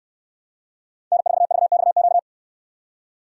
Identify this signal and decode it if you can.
Morse code, heard as a keyed tone, about 68 wpm, 690 Hz. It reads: N4VXQ